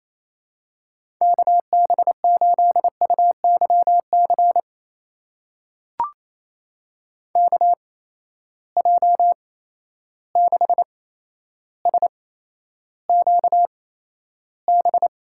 Morse code, heard as a keyed tone, about 28 wpm, spaced as for 12 wpm, 700 Hz. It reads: KB8UYC KJ6SQB
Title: USA Call Signs - 6 Characters - 8x Character Spacing - 28wpm